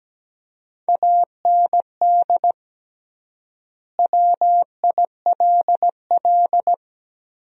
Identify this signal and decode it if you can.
Morse code, heard as a keyed tone, about 17 wpm, 700 Hz. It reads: AND WILL
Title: 3x Word-Spacing - Sentences from Top 400 Words - 17wpm